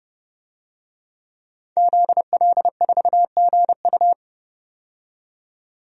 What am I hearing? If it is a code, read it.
Morse code, heard as a keyed tone, about 30 wpm, 700 Hz. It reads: ZL4GU
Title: DXCC-100 Call Signs - 5 Characters - 30wpm